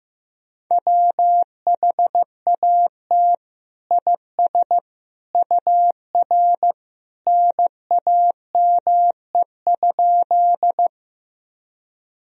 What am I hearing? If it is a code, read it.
Morse code, heard as a keyed tone, about 15 wpm, 700 Hz. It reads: WHAT IS UR NAME?
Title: Sentences from Top 200 Words and Abbreviations 15wpm